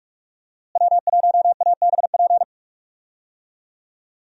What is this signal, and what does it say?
Morse code, heard as a keyed tone, about 43 wpm, 700 Hz. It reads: W1ABP
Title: USA Call Signs - 5 Characters - Repeated 2x - 45wpm